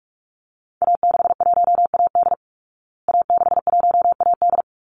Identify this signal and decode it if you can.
Morse code, heard as a keyed tone, about 44 wpm, 700 Hz. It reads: A61AD A61AD
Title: Call Signs Repeated 3x - 45wpm